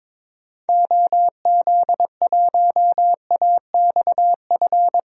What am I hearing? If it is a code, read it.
Morse code, heard as a keyed tone, about 21 wpm, 700 Hz. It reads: OZ1AXF
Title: DXCC-100 Call Signs - 6 Characters - 22wpm